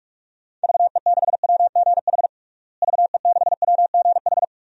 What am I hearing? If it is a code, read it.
Morse code, heard as a keyed tone, about 41 wpm, 700 Hz. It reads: VE6WGH VE6WGH
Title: DXCC-100 Call Signs - 6 Characters - Repeated 2x - 45wpm